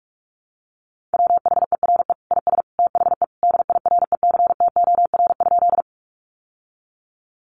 Morse code, heard as a keyed tone, about 44 wpm, 700 Hz. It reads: WHERE IS THE DIRECTOR?
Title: Sentences from Top 1300 Words 45wpm